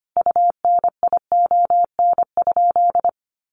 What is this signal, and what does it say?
Morse code, heard as a keyed tone, about 25 wpm, 700 Hz. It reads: UNION?
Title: Sentences from Top 1400 Words 25wpm